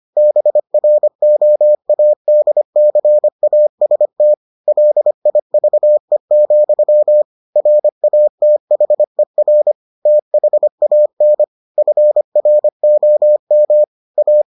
Morse code, heard as a keyed tone, about 25 wpm, 600 Hz. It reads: BROADCAST LIVE, RATHER THAN FROM A